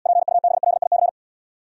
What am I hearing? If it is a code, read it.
Morse code, heard as a keyed tone, about 76 wpm, 700 Hz. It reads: PUBLIC